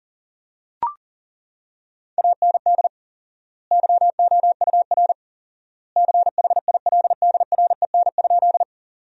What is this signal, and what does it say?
Morse code, heard as a keyed tone, about 40 wpm, 700 Hz. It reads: AND YOUR CHILDREN?
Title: 4x Word-Spacing - Sentences from Top 300 Words - 40wpm